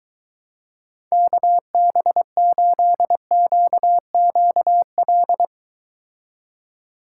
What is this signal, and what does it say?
Morse code, heard as a keyed tone, about 24 wpm, 700 Hz. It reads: KB8QQL